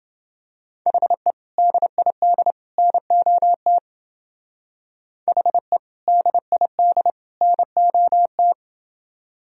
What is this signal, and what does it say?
Morse code, heard as a keyed tone, about 28 wpm, 700 Hz. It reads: HE DID NOT HE DID NOT